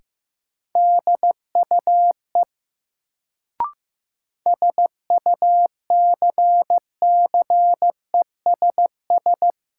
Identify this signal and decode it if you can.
Morse code, heard as a keyed tone, about 15 wpm, 700 Hz. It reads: DUE SUCCESS
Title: Sentences from Top 900 Words - New Word Drill - 15wpm